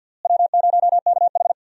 Morse code, heard as a keyed tone, about 47 wpm, 700 Hz. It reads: W0KH
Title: USA Call Signs - 4 Characters - Repeated 2x - 50wpm